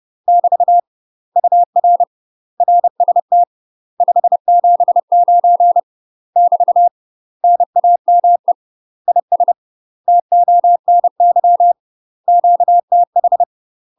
Morse code, heard as a keyed tone, about 30 wpm, 700 Hz. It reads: <BT> UR RST 579 <BT> NAME IS TONY QTH
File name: UK MASTER QSOs TRAINING / 30WPM / UK MASTER QSOs 0005